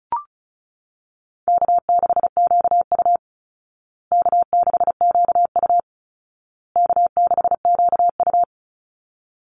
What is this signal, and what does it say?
Morse code, heard as a keyed tone, about 34 wpm, 700 Hz. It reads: K6QU K6QU K6QU
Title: Easy Call Signs Repeated 3x - 35wpm